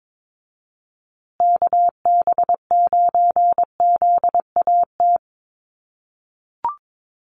Morse code, heard as a keyed tone, about 22 wpm, 700 Hz. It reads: KB9ZAT
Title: USA Call Signs Repeated 3x 22wpm